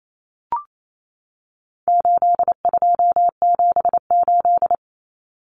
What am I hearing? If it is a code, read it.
Morse code, heard as a keyed tone, about 28 wpm, 700 Hz. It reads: 8278
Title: Numbers - 4-Digit 28wpm